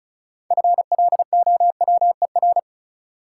Morse code, heard as a keyed tone, about 34 wpm, 700 Hz. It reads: FLOWER
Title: Six Letter Words - 35wpm